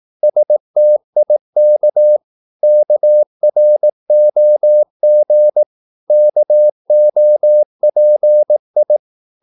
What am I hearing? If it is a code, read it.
Morse code, heard as a keyed tone, about 18 wpm, 600 Hz. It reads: STIK KROG KOPI
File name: Korte ord DK 18wpm